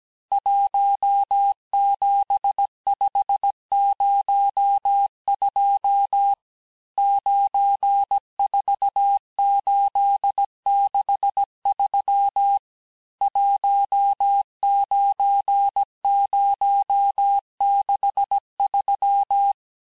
Morse code, heard as a keyed tone, about 17 wpm, 800 Hz. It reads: 17502 94863 19063
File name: cours-du-lundi files / cours-1587-texte-chiffre-vitesse-16MPM